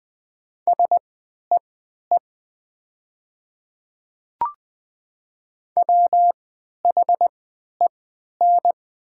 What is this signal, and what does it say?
Morse code, heard as a keyed tone, about 20 wpm, spaced as for 11 wpm, 700 Hz. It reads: SEE WHEN
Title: 3x Character Spacing - Top 200 Words - 20wpm